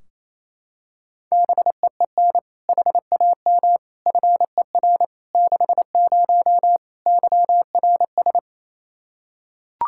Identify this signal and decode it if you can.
Morse code, heard as a keyed tone, about 28 wpm, 700 Hz. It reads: BEEN HAM FER 60 YRS
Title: Practice QSO - Exchange 3 - More Info - 28wpm